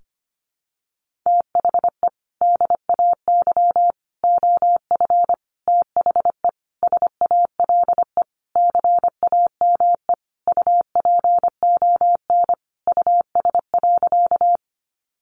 Morse code, heard as a keyed tone, about 25 wpm, 700 Hz. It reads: THE DAY OF THE SALE CAME UPON US.
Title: Sentences from Top 1400 Words 25wpm